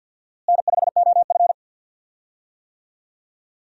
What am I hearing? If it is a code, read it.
Morse code, heard as a keyed tone, about 46 wpm, 700 Hz. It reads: N5OF
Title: USA Call Signs - 4 Characters - Repeated 3x - 50wpm